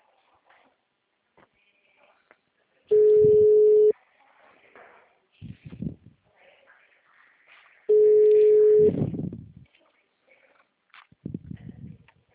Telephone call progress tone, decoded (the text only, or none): ringback tone